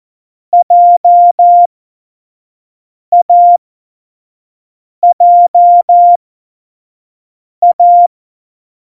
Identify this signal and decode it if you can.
Morse code, heard as a keyed tone, about 14 wpm, spaced as for 5 wpm, 700 Hz. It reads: JAJA